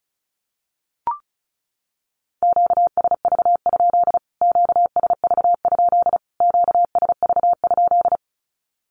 Morse code, heard as a keyed tone, about 34 wpm, 700 Hz. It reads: QSV? QSV? QSV?
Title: Q-Codes Repeated 3x 35wpm